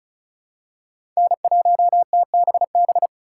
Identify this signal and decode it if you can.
Morse code, heard as a keyed tone, about 36 wpm, 700 Hz. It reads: N1TBB
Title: USA Call Signs - 5 Characters - 4x Character Spacing - 35wpm